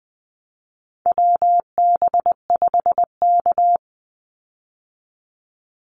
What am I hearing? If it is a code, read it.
Morse code, heard as a keyed tone, about 20 wpm, 700 Hz. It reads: WB5K
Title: Call Signs Repeated 2x - 20wpm